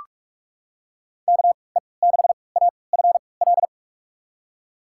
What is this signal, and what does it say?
Morse code, heard as a keyed tone, about 46 wpm, spaced as for 25 wpm, 700 Hz. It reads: KE6AFL
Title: USA Call Signs - 6 Characters - 3x Character Spacing - 45wpm